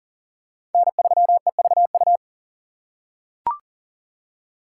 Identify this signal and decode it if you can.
Morse code, heard as a keyed tone, about 40 wpm, 700 Hz. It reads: N3EVU
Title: USA Call Signs - 4x Character Spacing - 40wpm